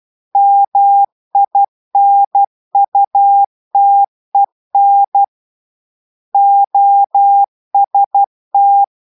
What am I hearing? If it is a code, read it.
Morse code, heard as a keyed tone, about 12 wpm, 800 Hz. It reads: MINUTEN OST